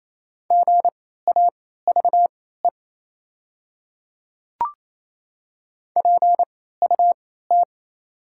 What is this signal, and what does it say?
Morse code, heard as a keyed tone, about 28 wpm, spaced as for 16 wpm, 700 Hz. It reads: GAVE PUT
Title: 3x Character Spacing - Top 500 Words - 28wpm